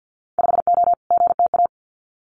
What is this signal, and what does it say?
Morse code, heard as a keyed tone, about 50 wpm, 700 Hz. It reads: 5O GTA
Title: Winter Field Day Exchanges Encore 50wpm